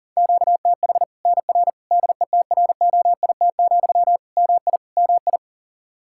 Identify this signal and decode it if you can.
Morse code, heard as a keyed tone, about 40 wpm, 700 Hz. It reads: QTH NR DETROIT, MI MI
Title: Practice QSO - Exchange 1 - Beginning - 40wpm